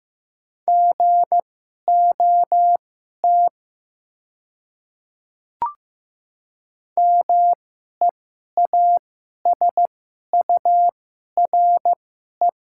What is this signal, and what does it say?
Morse code, heard as a keyed tone, about 15 wpm, spaced as for 11 wpm, 700 Hz. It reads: GOT MEASURE
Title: 2x Character Spacing - Top 500 Words - 15wpm